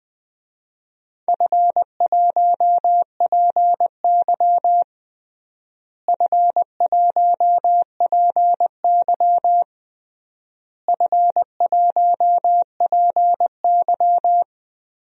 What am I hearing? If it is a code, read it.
Morse code, heard as a keyed tone, about 21 wpm, 700 Hz. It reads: F1PY F1PY F1PY
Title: DXCC-100 Call Signs - 4 Characters - Repeated 3x - 20wpm